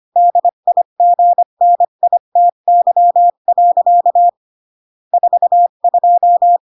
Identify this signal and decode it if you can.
Morse code, heard as a keyed tone, about 25 wpm, 700 Hz. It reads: DIGNITY. 42